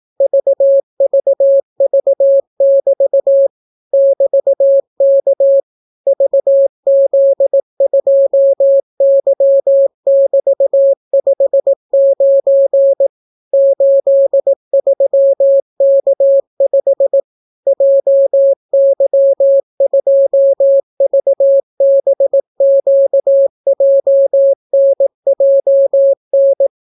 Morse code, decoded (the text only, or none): VVV<BT> <BT>K VZ2Y<BT>59 83K5 JY2VBQJNJN